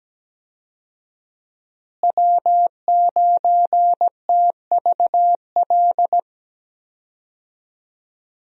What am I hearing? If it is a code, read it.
Morse code, heard as a keyed tone, about 17 wpm, 700 Hz. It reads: W9TVL
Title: USA Call Signs - 5 Characters - Repeated 2x - 17wpm